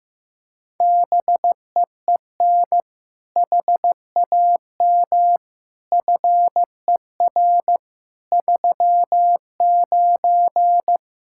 Morse code, heard as a keyed tone, about 15 wpm, 700 Hz. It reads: BEEN HAM FER 39
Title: Practice QSO - Exchange 3 - More Info - 15wpm